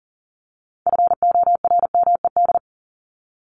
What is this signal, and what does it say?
Morse code, heard as a keyed tone, about 40 wpm, 700 Hz. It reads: FORMED
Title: ED Suffix - Top 100 Words with Base Words 40wpm